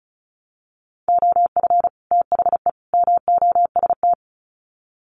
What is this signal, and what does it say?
Morse code, heard as a keyed tone, about 34 wpm, 700 Hz. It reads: OF THE MOST
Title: Sets of 3 Words 35wpm